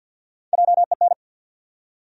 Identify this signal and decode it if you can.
Morse code, heard as a keyed tone, about 49 wpm, 700 Hz. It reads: JEN
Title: USA Common Names - Max 5 Letters - 50wpm